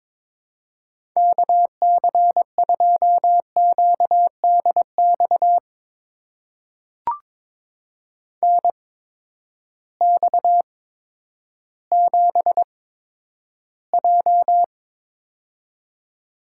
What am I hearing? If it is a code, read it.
Morse code, heard as a keyed tone, about 22 wpm, spaced as for 11 wpm, 700 Hz. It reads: KC2QDX NX7J
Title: USA Call Signs - 8x Character Spacing - 22wpm